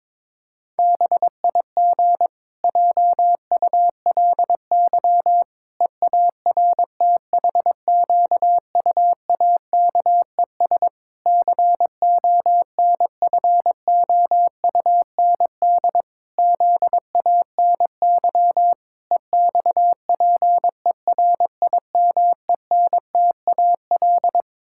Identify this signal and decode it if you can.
Morse code, heard as a keyed tone, about 22 wpm, 700 Hz. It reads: BIG JULY EARTHQUAKES CONFOUND ZANY EXPERIMENTAL
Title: Pangram Challenge v2 - 22wpm